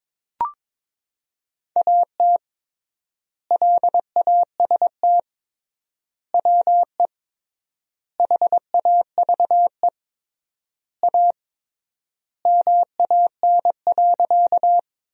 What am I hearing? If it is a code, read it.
Morse code, heard as a keyed tone, about 22 wpm, 700 Hz. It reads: AT LAST WE HAVE A MAN.
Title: 3x Word-Spacing - Sentences from Top 300 Words - 22wpm